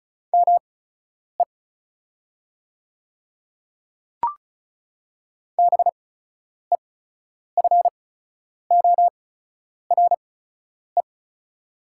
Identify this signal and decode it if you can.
Morse code, heard as a keyed tone, about 36 wpm, spaced as for 10 wpm, 690 Hz. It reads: ME BEFORE
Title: 8x Character Spacing - Top 500 Words - 35wpm